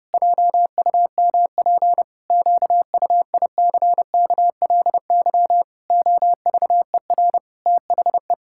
Morse code, heard as a keyed tone, about 30 wpm, 700 Hz. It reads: JUMP QUICKLY OVER THE
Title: Pangram Challenge v2 - 30wpm